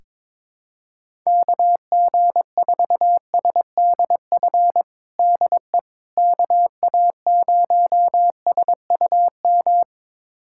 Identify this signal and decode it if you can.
Morse code, heard as a keyed tone, about 22 wpm, 700 Hz. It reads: KG4SDF DE KA0SUM
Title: USA Callsign DE Callsign - 6 Characters - 22wpm